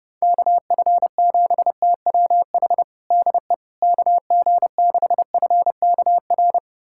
Morse code, heard as a keyed tone, about 32 wpm, 700 Hz. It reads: KF7TWH DE KG6FKR
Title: USA Callsign DE Callsign - 6 Characters - 30wpm